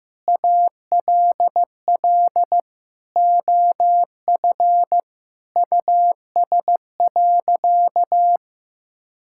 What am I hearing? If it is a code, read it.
Morse code, heard as a keyed tone, about 15 wpm, 700 Hz. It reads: ALL OF US.
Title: Sentences from Top 1000 Words - Review Part 6 - 15wpm